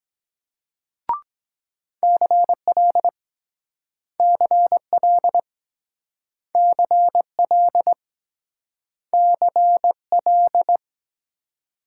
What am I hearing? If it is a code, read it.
Morse code, heard as a keyed tone, about 24 wpm, 700 Hz. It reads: CL CL CL CL